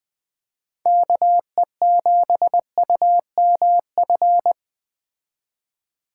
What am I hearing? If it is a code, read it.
Morse code, heard as a keyed tone, about 20 wpm, 700 Hz. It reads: KE7UMF